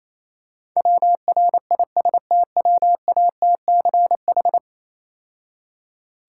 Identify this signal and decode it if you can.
Morse code, heard as a keyed tone, about 27 wpm, 690 Hz. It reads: WRISTWATCH
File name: Easy Words - Speed-Racing with Hints - 22wpm